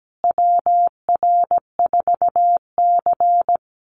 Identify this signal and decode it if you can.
Morse code, heard as a keyed tone, about 17 wpm, 700 Hz. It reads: WR4C
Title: USA Call Signs Repeated 3x 17wpm